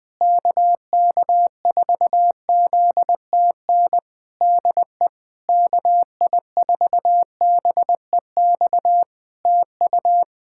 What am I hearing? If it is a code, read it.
Morse code, heard as a keyed tone, about 20 wpm, 700 Hz. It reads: KK4ZTN DE KI4BEX TU
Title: K1USN SST QSO Practice 20wpm